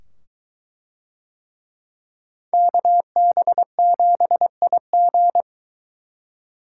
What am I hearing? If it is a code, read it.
Morse code, heard as a keyed tone, about 22 wpm, 700 Hz. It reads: KB7IG